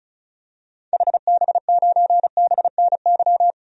Morse code, heard as a keyed tone, about 34 wpm, 700 Hz. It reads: HB9BNY
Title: DXCC-100 Call Signs - 6 Characters - 35wpm